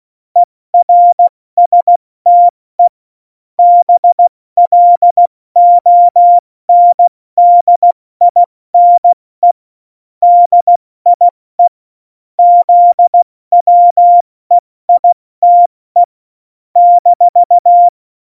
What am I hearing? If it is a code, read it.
Morse code, heard as a keyed tone, about 16 wpm, 700 Hz. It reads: ERSTE BLONDINE DIE ZWEITE -
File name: Morsekurs / Witze / Witz 18 Juli 17, Tempo 70 F80